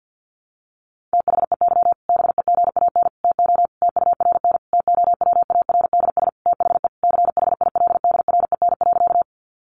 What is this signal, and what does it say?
Morse code, heard as a keyed tone, about 50 wpm, 700 Hz. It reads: THEY BEGAN TO TURN TOWARDS THE CHILDREN.
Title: Sentences from Top 1100 Words 50wpm